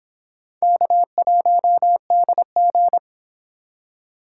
Morse code, heard as a keyed tone, about 26 wpm, 700 Hz. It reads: K1DG